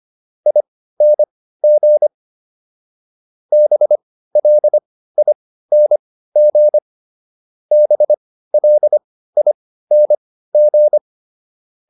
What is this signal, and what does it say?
Morse code, heard as a keyed tone, about 25 wpm, spaced as for 15 wpm, 600 Hz. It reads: ING BLING BLING